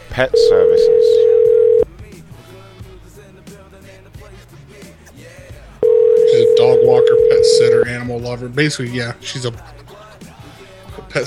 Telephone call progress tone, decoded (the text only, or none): ringback tone